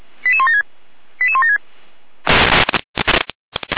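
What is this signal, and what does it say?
Five-tone selective call (ZVEI), heard as quick runs of five tones, tones of about 70 ms. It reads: 88176 88176
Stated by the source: ZVEI five-tone call 88179, received on SDR.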